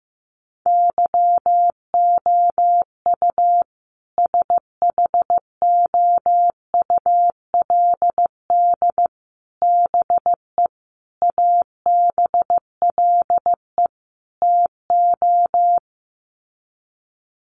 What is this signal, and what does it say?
Morse code, heard as a keyed tone, about 15 wpm, 700 Hz. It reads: YOU SHOULD BE ABLE TO
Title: Sets of 5 Words Encore 15wpm